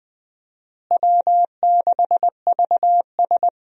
Morse code, heard as a keyed tone, about 20 wpm, 700 Hz. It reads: W6VS
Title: USA Call Signs - 4 Characters - Repeated 3x - 20wpm